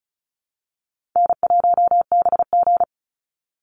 Morse code, heard as a keyed tone, about 34 wpm, 700 Hz. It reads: N1BG
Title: Call Signs Repeated 2x - 35wpm